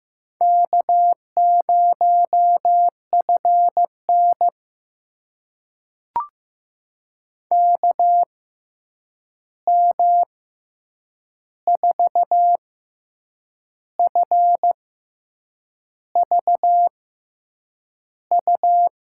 Morse code, heard as a keyed tone, about 15 wpm, spaced as for 7 wpm, 700 Hz. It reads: K0FN KM4FVU
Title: USA Call Signs - 6x Character Spacing - 15wpm